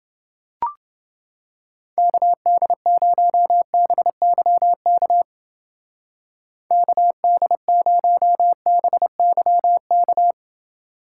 Morse code, heard as a keyed tone, about 28 wpm, 700 Hz. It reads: KD0BYK KD0BYK